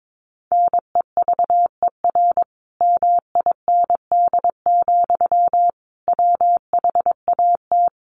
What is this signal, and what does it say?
Morse code, heard as a keyed tone, about 22 wpm, 700 Hz. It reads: NEVER MIND, WHAT
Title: Sentences from Top 700 Words 22wpm